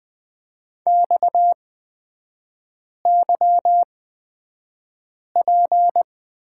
Morse code, heard as a keyed tone, about 20 wpm, spaced as for 5 wpm, 700 Hz. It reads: XYP
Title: 255 Lesson 41e - Period - Three Characters 20wpm